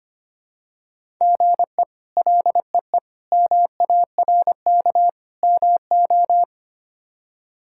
Morse code, heard as a keyed tone, about 25 wpm, 700 Hz. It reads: GE LEE MARK MO
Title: K1USN SST QSO Practice - 25wpm